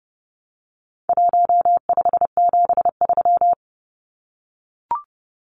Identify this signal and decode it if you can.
Morse code, heard as a keyed tone, about 30 wpm, 700 Hz. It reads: 1573
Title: Numbers - Contest Style - 4-Digit 30wpm